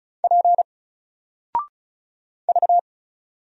Morse code, heard as a keyed tone, about 36 wpm, 700 Hz. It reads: P V
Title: Single Letter-Number 35wpm